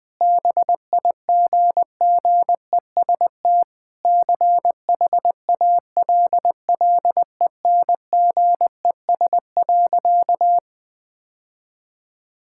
Morse code, heard as a keyed tone, about 20 wpm, 700 Hz. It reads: BIGGEST CHALLENGES.
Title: Sentences with 4-Word Phrases - 20wpm